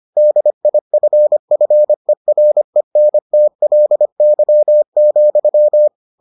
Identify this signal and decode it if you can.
Morse code, heard as a keyed tone, about 25 wpm, 600 Hz. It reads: DIFFERENTLY,